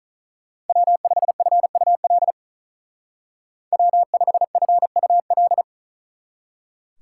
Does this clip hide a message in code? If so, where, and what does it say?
Yes: Morse code from 0.7 to 5.6 s, heard as a keyed tone, about 45 wpm, 700 Hz. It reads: W5FUL W5FUL